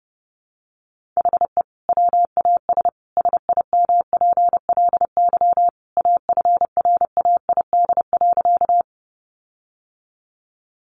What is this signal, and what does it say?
Morse code, heard as a keyed tone, about 30 wpm, 700 Hz. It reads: HE WAS SIMPLY AFRAID.
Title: Sentences from Top 1500 Words 30wpm